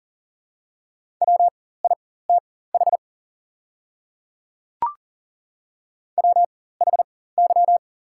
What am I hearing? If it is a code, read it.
Morse code, heard as a keyed tone, about 42 wpm, spaced as for 19 wpm, 690 Hz. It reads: WITH WHY
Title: 4x Character Spacing - Top 200 Words - 40wpm